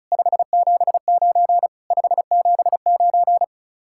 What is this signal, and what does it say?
Morse code, heard as a keyed tone, about 44 wpm, 700 Hz. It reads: 579 579
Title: Speed-Racing Common QSO Phrases - 35wpm